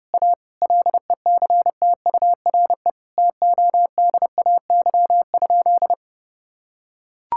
Morse code, heard as a keyed tone, about 32 wpm, 700 Hz. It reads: A LECTURE TODAY?
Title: Sentences from Top 1600 Words 30wpm